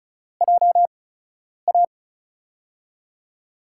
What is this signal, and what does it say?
Morse code, heard as a keyed tone, about 36 wpm, 700 Hz. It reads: J A
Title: USA Call Signs - 5 Characters - 8x Character Spacing - 35wpm